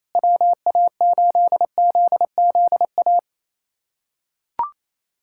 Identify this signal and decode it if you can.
Morse code, heard as a keyed tone, about 28 wpm, 700 Hz. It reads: WA8ZZA
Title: USA Callsign DE Callsign - 6 Characters - 28wpm